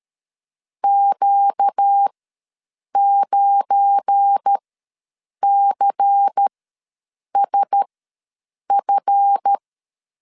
Morse code, heard as a keyed tone, about 13 wpm, spaced as for 7 wpm, 800 Hz. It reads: Q9CSF
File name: LEZIONI / 7 GF9Z / 7 GF9 ws 12wpm 3344